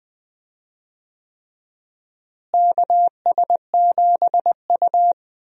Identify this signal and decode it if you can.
Morse code, heard as a keyed tone, about 20 wpm, 700 Hz. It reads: KS7U
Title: USA Call Signs - 4 Characters - Repeated 3x - 20wpm